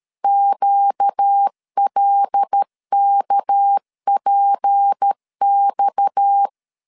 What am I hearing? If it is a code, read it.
Morse code, heard as a keyed tone, about 13 wpm, 800 Hz. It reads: QLKPX
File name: LEZIONI / 7 GF9Z / 7 GF9 12wpm 2319